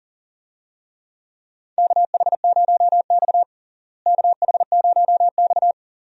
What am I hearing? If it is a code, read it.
Morse code, heard as a keyed tone, about 40 wpm, 700 Hz. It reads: KH0X KH0X
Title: USA Call Signs - 4 Characters - Repeated 2x - 40wpm